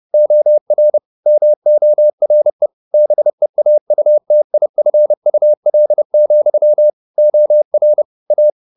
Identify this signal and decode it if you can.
Morse code, heard as a keyed tone, about 30 wpm, 600 Hz. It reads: OR MORE BEAUTIFUL, OR A